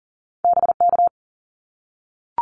Morse code, heard as a keyed tone, about 40 wpm, 700 Hz. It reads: BK
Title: Abbreviations 40wpm